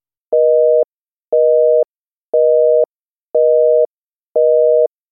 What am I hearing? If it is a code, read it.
Busy tone (North America), 480 + 620 Hz, with noise.
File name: Telephone tone redial 7 numbers busy